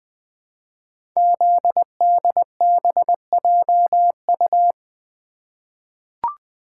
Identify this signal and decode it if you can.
Morse code, heard as a keyed tone, about 20 wpm, 700 Hz. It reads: ZDBJU
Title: Code Groups - 5 Letters - 20wpm